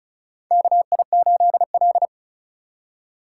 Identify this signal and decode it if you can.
Morse code, heard as a keyed tone, about 36 wpm, 700 Hz. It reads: KI8L
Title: USA Call Signs - 4 Characters - 4x Character Spacing - 35wpm